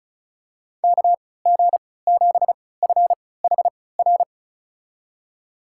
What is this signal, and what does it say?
Morse code, heard as a keyed tone, about 36 wpm, spaced as for 20 wpm, 690 Hz. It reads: KG7FHR
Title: USA Call Signs - 3x Character Spacing - 35wpm